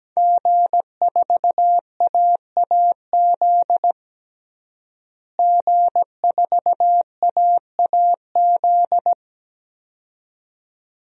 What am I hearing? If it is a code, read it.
Morse code, heard as a keyed tone, about 17 wpm, 700 Hz. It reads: G4AAZ G4AAZ
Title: DXCC-100 Call Signs - 5 Characters - Repeated 2x - 17wpm